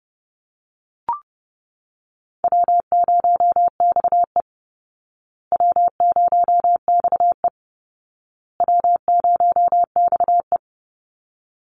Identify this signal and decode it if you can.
Morse code, heard as a keyed tone, about 30 wpm, 700 Hz. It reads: W0XE W0XE W0XE
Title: Easy Call Signs Repeated 3x - 30wpm